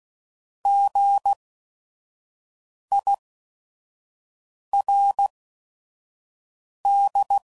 Morse code, heard as a keyed tone, about 16 wpm, spaced as for 5 wpm, 800 Hz. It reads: GIRD
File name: Prefixes / By First Letter / Gs